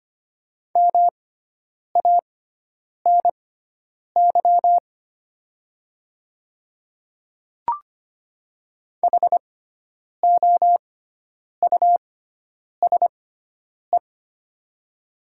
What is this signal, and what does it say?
Morse code, heard as a keyed tone, about 25 wpm, spaced as for 9 wpm, 700 Hz. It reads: MANY HOUSE
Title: 6x Character Spacing - Top 200 Words - 25wpm